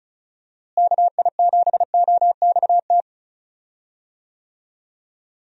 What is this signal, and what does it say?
Morse code, heard as a keyed tone, about 36 wpm, 700 Hz. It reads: KI7OXT
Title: USA Call Signs - 6 Characters - Repeated 2x - 35wpm